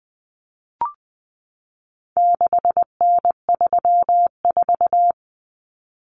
Morse code, heard as a keyed tone, about 19 wpm, 700 Hz. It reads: 6N34
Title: Common Cut Numbers - 4-Digit 20wpm